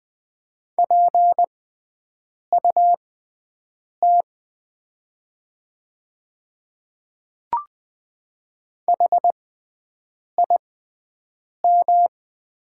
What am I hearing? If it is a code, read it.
Morse code, heard as a keyed tone, about 20 wpm, spaced as for 7 wpm, 700 Hz. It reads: PUT HIM
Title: 6x Character Spacing - Top 300 Words - 20wpm